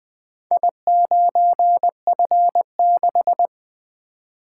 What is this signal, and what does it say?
Morse code, heard as a keyed tone, about 21 wpm, 700 Hz. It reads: I9F6
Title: Four Letter-Number Combinations - 20wpm